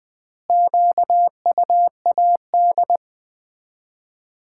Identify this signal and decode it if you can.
Morse code, heard as a keyed tone, about 20 wpm, 700 Hz. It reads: QUAD
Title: Common QSO Phrases - 4x Word-Spacing - 20wpm